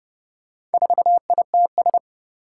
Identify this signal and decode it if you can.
Morse code, heard as a keyed tone, about 30 wpm, 700 Hz. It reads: 4ITS
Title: Four Challenging Letter-Number Combinations 30wpm